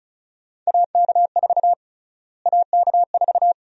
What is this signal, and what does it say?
Morse code, heard as a keyed tone, about 34 wpm, 700 Hz. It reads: AK4 AK4
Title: USA Call Sign Prefixes - Repeated 2x - 35wpm